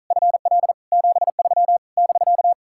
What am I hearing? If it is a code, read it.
Morse code, heard as a keyed tone, about 46 wpm, 700 Hz. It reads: FL 73 <BK>